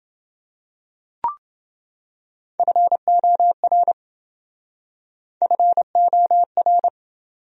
Morse code, heard as a keyed tone, about 27 wpm, 700 Hz. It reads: FOR FOR